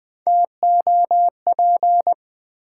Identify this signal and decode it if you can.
Morse code, heard as a keyed tone, about 20 wpm, 700 Hz. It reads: TOP
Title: Sentences from Top 700 Words - New Word Drill - 20wpm